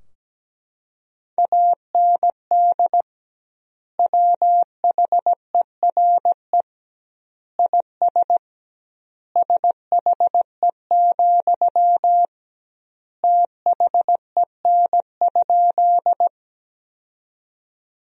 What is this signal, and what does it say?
Morse code, heard as a keyed tone, about 17 wpm, 700 Hz. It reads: AND WHERE IS SHE, THEN?
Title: 2x Word-Spacing - Sentences from Top 200 Words - 17wpm